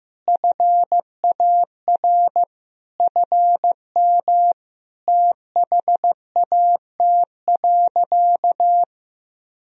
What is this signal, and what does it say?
Morse code, heard as a keyed tone, about 15 wpm, 700 Hz. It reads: FAR FM THAT.
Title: Sentences from Top 300 Words and Abbreviations 15wpm